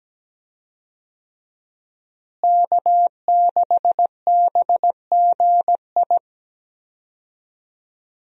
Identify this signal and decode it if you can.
Morse code, heard as a keyed tone, about 17 wpm, 700 Hz. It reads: K6BGI
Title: USA Call Signs - 5 Characters - Repeated 2x - 17wpm